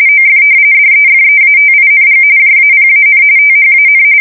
RTTY, about 45.45 baud. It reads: OF DEFEATING JAPAN THAN
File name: fsk170 / usnavy4